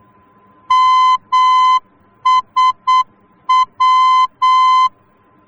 Morse code, heard as a keyed tone, about 8 wpm, 1020 Hz. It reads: MSW
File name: germany / mx / MSW-D